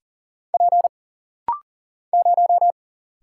Morse code, heard as a keyed tone, about 40 wpm, 700 Hz. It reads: P 0
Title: Single Letter-Number - Rapid-Fire 40wpm